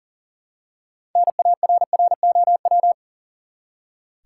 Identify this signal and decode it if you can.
Morse code, heard as a keyed tone, about 40 wpm, 700 Hz. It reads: NARROW
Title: Six Letter Words - 40wpm